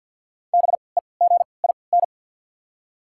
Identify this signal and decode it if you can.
Morse code, heard as a keyed tone, about 53 wpm, spaced as for 29 wpm, 690 Hz. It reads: BEGIN
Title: 3x Character Spacing - Top 300 Words - 50wpm